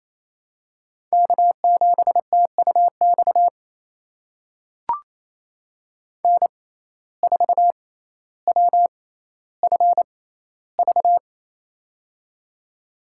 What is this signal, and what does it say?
Morse code, heard as a keyed tone, about 28 wpm, spaced as for 14 wpm, 700 Hz. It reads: K7TUX N4WFV
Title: USA Call Signs - 5 Characters - 6x Character Spacing - 28wpm